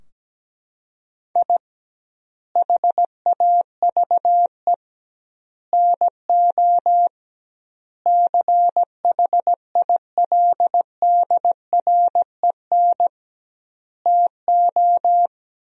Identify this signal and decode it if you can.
Morse code, heard as a keyed tone, about 17 wpm, 700 Hz. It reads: I HAVE NO CHILDREN TO
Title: 2x Word-Spacing - Sentences from Top 400 Words - 17wpm